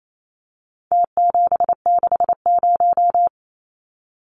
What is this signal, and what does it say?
Morse code, heard as a keyed tone, about 28 wpm, 700 Hz. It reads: T760
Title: Numbers - Contest Style - 4-Digit 28wpm